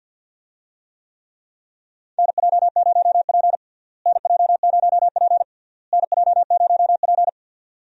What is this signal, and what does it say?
Morse code, heard as a keyed tone, about 50 wpm, 700 Hz. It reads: NJ0P NJ0P NJ0P
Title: USA Call Signs - 4 Characters - Repeated 3x - 50wpm